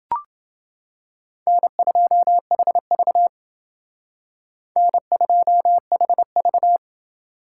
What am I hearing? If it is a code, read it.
Morse code, heard as a keyed tone, about 28 wpm, 700 Hz. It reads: N2HV N2HV